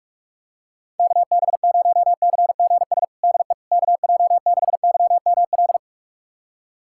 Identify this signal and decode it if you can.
Morse code, heard as a keyed tone, about 47 wpm, 700 Hz. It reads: KB0CGS DE KJ6YML
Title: USA Callsign DE Callsign - 6 Characters - 45wpm